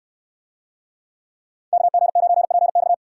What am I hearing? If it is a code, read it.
Morse code, heard as a keyed tone, about 69 wpm, 700 Hz. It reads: KG9PX